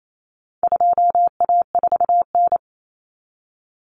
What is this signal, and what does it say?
Morse code, heard as a keyed tone, about 28 wpm, 700 Hz. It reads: 2A4N
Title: Common Cut Numbers - 4-Digit 28wpm